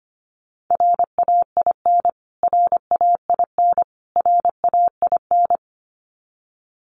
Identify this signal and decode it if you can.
Morse code, heard as a keyed tone, about 25 wpm, 700 Hz. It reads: RAIN RAIN RAIN
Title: Top 500 Words Repeated 25wpm